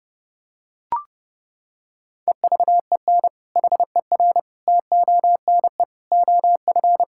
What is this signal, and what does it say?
Morse code, heard as a keyed tone, about 32 wpm, 700 Hz. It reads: EVEN HER TONE OF
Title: Sentences from Top 1600 Words 30wpm